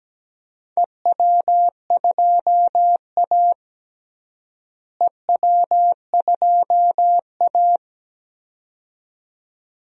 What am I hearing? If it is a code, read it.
Morse code, heard as a keyed tone, about 17 wpm, 700 Hz. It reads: EW2A EW2A
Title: DXCC-100 Call Signs - 4 Characters - Repeated 2x - 17wpm